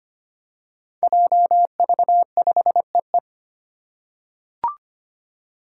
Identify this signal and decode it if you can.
Morse code, heard as a keyed tone, about 25 wpm, 700 Hz. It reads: JV5EE
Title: Five Challenging Letter-Number Combinations 25wpm